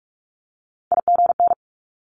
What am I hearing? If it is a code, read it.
Morse code, heard as a keyed tone, about 46 wpm, 690 Hz. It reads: IGN
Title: Multi-letter Phonemes 45wpm